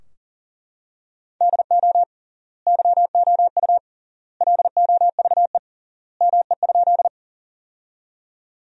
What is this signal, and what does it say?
Morse code, heard as a keyed tone, about 40 wpm, 700 Hz. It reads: DO YOU LOVE ME?
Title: 3x Word-Spacing - Sentences from Top 400 Words - 40wpm